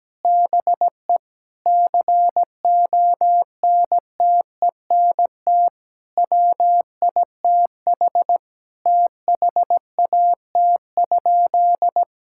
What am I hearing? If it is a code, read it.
Morse code, heard as a keyed tone, about 17 wpm, 700 Hz. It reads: BE CONTENT WITH THAT?
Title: Sentences from Top 1000 Words - Review Part 6 - 17wpm